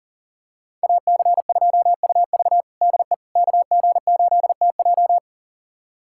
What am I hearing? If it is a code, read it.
Morse code, heard as a keyed tone, about 37 wpm, 700 Hz. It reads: AC2UV DE KG8TJ
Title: USA Callsign DE Callsign - 5 Characters - 40wpm